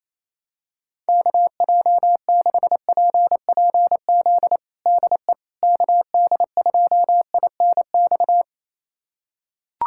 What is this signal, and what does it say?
Morse code, heard as a keyed tone, about 28 wpm, 700 Hz. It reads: KJ6PPZ DE KD2INX
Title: USA Callsign DE Callsign 28wpm